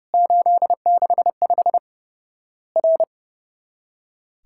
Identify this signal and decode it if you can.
Morse code, heard as a keyed tone, about 32 wpm, 700 Hz. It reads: 865 R
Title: Worked All Europe DX Contest - QTC Exchange - 30wpm